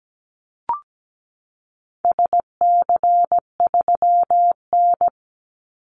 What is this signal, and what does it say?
Morse code, heard as a keyed tone, about 17 wpm, 700 Hz. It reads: SC3N
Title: Call Signs - 17wpm